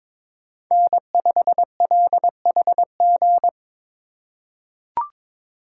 Morse code, heard as a keyed tone, about 22 wpm, 700 Hz. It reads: N5LHG
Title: USA Call Signs - 4x Character Spacing - 22wpm